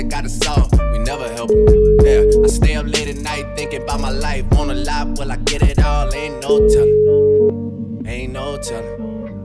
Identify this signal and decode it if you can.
Ringback tone (Europe, 425 Hz), with speech.